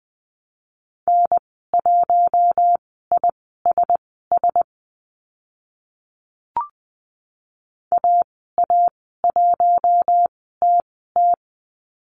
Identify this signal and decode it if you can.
Morse code, heard as a keyed tone, about 20 wpm, spaced as for 14 wpm, 700 Hz. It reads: N1ISS AA1TT
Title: 042 Lesson 8g - Number 1 - Call Signs 20wpm